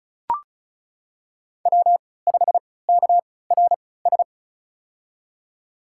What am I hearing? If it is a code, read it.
Morse code, heard as a keyed tone, about 36 wpm, spaced as for 20 wpm, 690 Hz. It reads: W5KRS
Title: USA Call Signs - 5 Characters - 3x Character Spacing - 35wpm